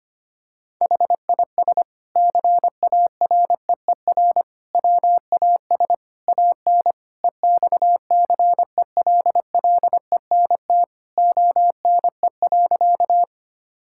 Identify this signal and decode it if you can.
Morse code, heard as a keyed tone, about 25 wpm, 700 Hz. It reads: HIS CAREER WAS AN EXCELLENT ONE.
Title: Sentences from Top 1000 Words - Review Part 6 - 25wpm